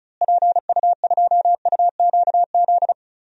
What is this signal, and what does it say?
Morse code, heard as a keyed tone, about 34 wpm, 700 Hz. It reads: PU2UQZ
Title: DXCC-100 Call Signs - 6 Characters - Repeated 3x - 35wpm